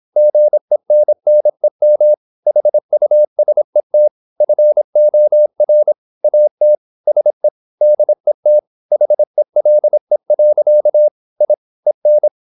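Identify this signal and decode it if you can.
Morse code, heard as a keyed tone, about 26 wpm, 600 Hz. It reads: GENNEM HUSET FOR AT SE DET HELE. I EN
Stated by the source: High-speed (QRQ) Morse at 26 wpm.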